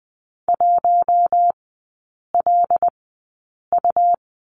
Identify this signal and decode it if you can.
Morse code, heard as a keyed tone, about 20 wpm, spaced as for 8 wpm, 700 Hz. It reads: 1LU
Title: 100 Lesson 17e - Letter u - Three Characters 20wpm